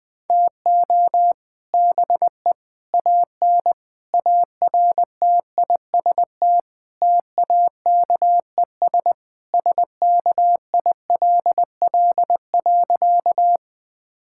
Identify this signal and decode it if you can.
Morse code, heard as a keyed tone, about 20 wpm, 700 Hz. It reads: TO BE AN ARTIST TAKES SKILL.
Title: Sentences with 3-Word Phrases - Spoken Beginning - 20wpm